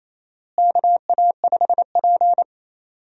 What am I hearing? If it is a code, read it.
Morse code, heard as a keyed tone, about 28 wpm, 700 Hz. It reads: KA5P
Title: USA Call Signs - 4 Characters - Repeated 2x - 28wpm